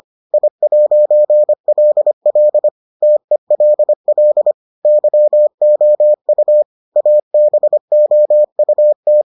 Morse code, heard as a keyed tone, about 25 wpm, 600 Hz. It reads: I'LL TELL YOU ABOUT